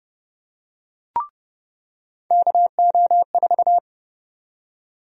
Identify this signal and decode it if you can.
Morse code, heard as a keyed tone, about 32 wpm, 700 Hz. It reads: KO4
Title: USA Call Sign Prefixes - 30wpm